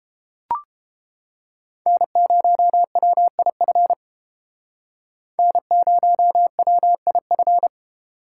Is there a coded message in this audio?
Yes: Morse code from 1.9 to 7.7 s, heard as a keyed tone, about 32 wpm, 700 Hz. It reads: N0WIF N0WIF